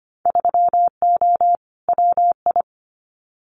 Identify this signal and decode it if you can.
Morse code, heard as a keyed tone, about 25 wpm, 700 Hz. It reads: 3O WI
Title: Winter Field Day Exchanges 25wpm